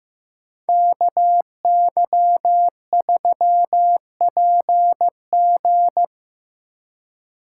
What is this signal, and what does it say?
Morse code, heard as a keyed tone, about 18 wpm, 700 Hz. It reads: KY3PG
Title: Speed Racing - USA 2x2 Call Signs 15wpm